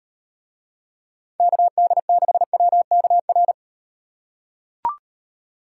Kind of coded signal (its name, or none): Morse code